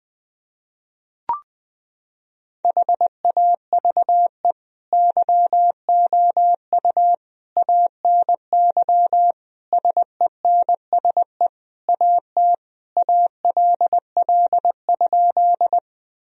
Morse code, heard as a keyed tone, about 20 wpm, 700 Hz. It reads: HAVE YOU ANY SENSE AT ALL?
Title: Sentences from Top 1000 Words - Review Part 4 - 20wpm